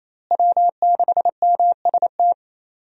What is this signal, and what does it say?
Morse code, heard as a keyed tone, about 28 wpm, 700 Hz. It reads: W6MST
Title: USA Call Signs - 5 Characters - Repeated 3x - 28wpm